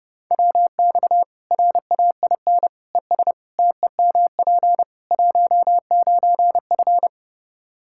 Morse code, heard as a keyed tone, about 30 wpm, 700 Hz. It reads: WX RAIN ES TEMP 19F
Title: Practice QSO - Exchange 2 - Info - 30wpm